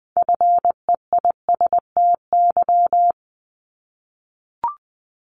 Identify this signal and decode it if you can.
Morse code, heard as a keyed tone, about 20 wpm, 700 Hz. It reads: FEISTY
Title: Multi-letter Phonemes Repeated 3x 20wpm